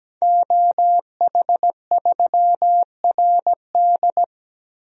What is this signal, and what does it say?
Morse code, heard as a keyed tone, about 17 wpm, 700 Hz. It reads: OH3RD
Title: DXCC-100 Call Signs - 5 Characters - Repeated 3x - 17wpm